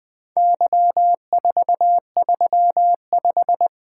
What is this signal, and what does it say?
Morse code, heard as a keyed tone, about 20 wpm, 700 Hz. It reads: Y435
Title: Four Letter-Number Combinations - 20wpm